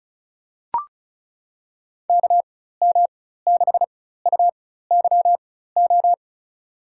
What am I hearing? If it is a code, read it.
Morse code, heard as a keyed tone, about 34 wpm, spaced as for 16 wpm, 690 Hz. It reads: KM6UYO
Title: USA Call Signs - 6 Characters - 4x Character Spacing - 35wpm